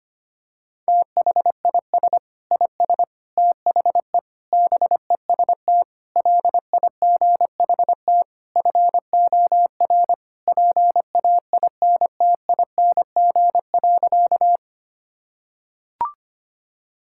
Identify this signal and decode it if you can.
Morse code, heard as a keyed tone, about 25 wpm, 700 Hz. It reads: THIS IS THE BEST LIGHT FOR PAINTING.
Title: Sentences from Top 1000 Words - Review Part 6 - 25wpm